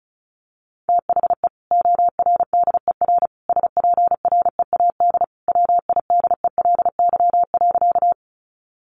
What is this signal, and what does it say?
Morse code, heard as a keyed tone, about 34 wpm, 700 Hz. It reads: THE ORDER SPREAD WIDELY.
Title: Sentences from Top 1200 Words 35wpm